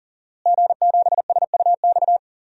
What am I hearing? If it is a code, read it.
Morse code, heard as a keyed tone, about 40 wpm, 700 Hz. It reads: G7SUX
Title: DXCC-100 Call Signs - 5 Characters - Repeated 2x - 40wpm